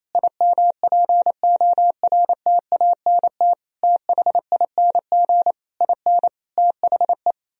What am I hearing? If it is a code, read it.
Morse code, heard as a keyed tone, about 28 wpm, 700 Hz. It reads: IMPORTANT THING IN THE
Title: Sentences from Top 1000 Words - Review Part 3 - 28wpm